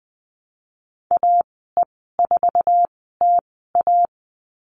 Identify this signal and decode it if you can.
Morse code, heard as a keyed tone, about 20 wpm, spaced as for 15 wpm, 700 Hz. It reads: AE4TA
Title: 048 Lesson 9g - Number 4 - Call Signs 20wpm